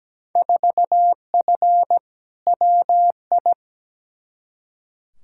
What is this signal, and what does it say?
Morse code, heard as a keyed tone, about 17 wpm, 700 Hz. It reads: 4F WI
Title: Field Day Exchanges - 17wpm